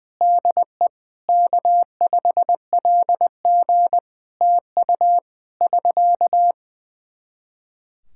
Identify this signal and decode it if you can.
Morse code, heard as a keyed tone, about 20 wpm, 700 Hz. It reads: DE K5LG TU <SK>
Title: K1USN SST QSO Practice 20wpm